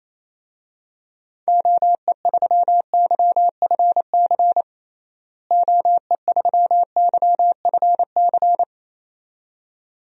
Morse code, heard as a keyed tone, about 27 wpm, 700 Hz. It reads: OE3YFC OE3YFC
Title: DXCC-100 Call Signs - 6 Characters - Repeated 2x - 28wpm